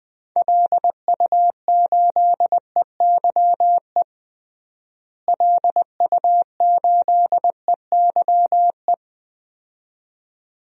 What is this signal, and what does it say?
Morse code, heard as a keyed tone, about 20 wpm, 700 Hz. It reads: LU8EYE LU8EYE
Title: DXCC-100 Call Signs - 6 Characters - Repeated 2x - 20wpm